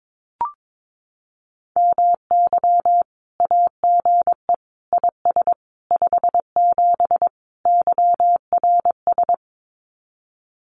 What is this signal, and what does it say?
Morse code, heard as a keyed tone, about 22 wpm, 700 Hz. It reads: MY AGE IS 57 YRS
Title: Common QSO Phrases Encore 22wpm